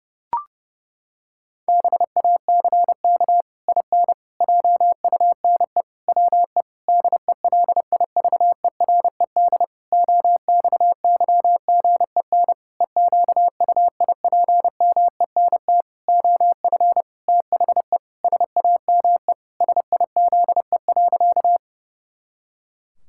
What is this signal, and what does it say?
Morse code, heard as a keyed tone, about 32 wpm, 700 Hz. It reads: BACK IN JUNE WE DELIVERED OXYGEN EQUIPMENT OF THE SAME SIZE.
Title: Pangram Challenge v2 - 30wpm